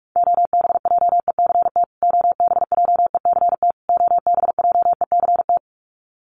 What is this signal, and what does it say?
Morse code, heard as a keyed tone, about 44 wpm, 700 Hz. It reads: OBJECT OBJECT OBJECT
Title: Top 500 Words Repeated 45wpm